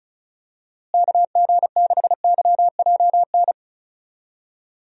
Morse code, heard as a keyed tone, about 36 wpm, 700 Hz. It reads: KG6YJN
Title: USA Call Signs - 6 Characters - Repeated 2x - 35wpm